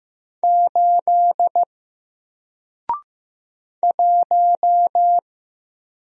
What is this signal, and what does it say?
Morse code, heard as a keyed tone, about 15 wpm, 700 Hz. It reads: 8 1
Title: Single Letter-Number Alt 15wpm